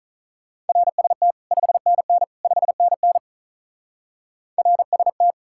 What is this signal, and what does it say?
Morse code, heard as a keyed tone, about 42 wpm, 700 Hz. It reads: RST 5NN 5NN RST
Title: Speed-Racing Common QSO Phrases - 35wpm